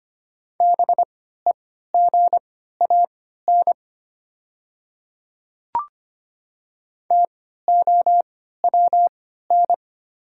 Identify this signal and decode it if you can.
Morse code, heard as a keyed tone, about 25 wpm, spaced as for 14 wpm, 700 Hz. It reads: BEGAN TOWN
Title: 3x Character Spacing - Top 500 Words - 25wpm